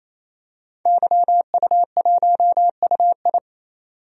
Morse code, heard as a keyed tone, about 28 wpm, 700 Hz. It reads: YU1UI
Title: DXCC-100 Call Signs - 5 Characters - 28wpm